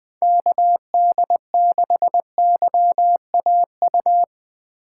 Morse code, heard as a keyed tone, about 20 wpm, 700 Hz. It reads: KD6YAU
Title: USA Call Signs - 2x Character Spacing - 20wpm